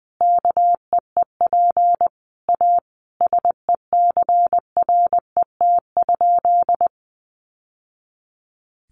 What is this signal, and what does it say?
Morse code, heard as a keyed tone, about 20 wpm, 700 Hz. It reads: KEEP A SECRET?
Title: Sentences from Top 1200 Words 20wpm